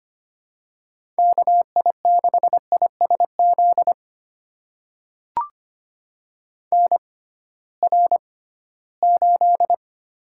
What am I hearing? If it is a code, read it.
Morse code, heard as a keyed tone, about 25 wpm, spaced as for 16 wpm, 700 Hz. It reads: KI6ISZ NR8
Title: USA Call Signs - 6x Character Spacing - 25wpm